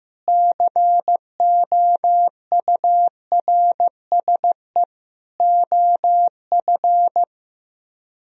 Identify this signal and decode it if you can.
Morse code, heard as a keyed tone, about 16 wpm, 700 Hz. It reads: COURSE OF
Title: Speed-Racing - Sets of 4 Words 15wpm